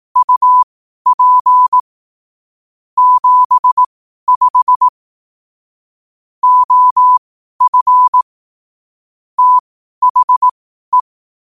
Morse code, heard as a keyed tone, about 18 wpm, spaced as for 12 wpm, 1000 Hz. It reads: UP 75 OF THE
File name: qotdfiles / 12wpm